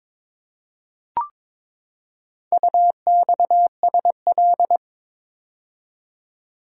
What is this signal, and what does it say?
Morse code, heard as a keyed tone, about 22 wpm, 700 Hz. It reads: UXSL
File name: practice / 22wpm / Four Letter-Number Combinations - 22wpm